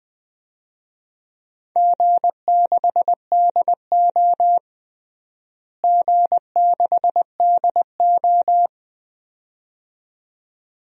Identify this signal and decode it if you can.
Morse code, heard as a keyed tone, about 20 wpm, 700 Hz. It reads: G6DO G6DO
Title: DXCC-100 Call Signs - 4 Characters - Repeated 2x - 20wpm